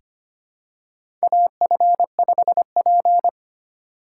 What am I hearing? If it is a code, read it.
Morse code, heard as a keyed tone, about 25 wpm, 700 Hz. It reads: AF5P
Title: K1USN SST QSO Practice - 25wpm